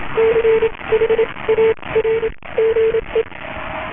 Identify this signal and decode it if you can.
Morse code, heard as a keyed tone, about 28 wpm, spaced as for 21 wpm, 450 Hz, with noise. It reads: CHARGE